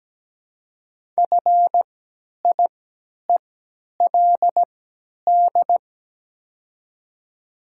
Morse code, heard as a keyed tone, about 17 wpm, spaced as for 10 wpm, 700 Hz. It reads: FIELD
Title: 3x Character Spacing - Top 500 Words - 17wpm